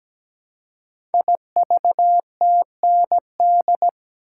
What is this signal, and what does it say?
Morse code, heard as a keyed tone, about 17 wpm, 700 Hz. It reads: IVTND
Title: Code Groups - 5 Letters - 17wpm